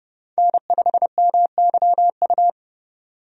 Morse code, heard as a keyed tone, about 31 wpm, 700 Hz. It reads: N5MYU
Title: USA Call Signs - 5 Characters - 30wpm